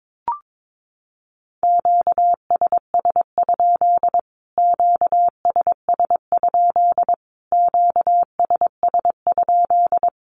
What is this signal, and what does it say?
Morse code, heard as a keyed tone, about 22 wpm, 700 Hz. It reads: QSS? QSS? QSS?
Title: Q-Codes Repeated 3x 22wpm